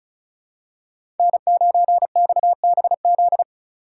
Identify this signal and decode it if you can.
Morse code, heard as a keyed tone, about 36 wpm, 700 Hz. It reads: N9XBZ
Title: USA Call Signs - 2x Character Spacing - 35wpm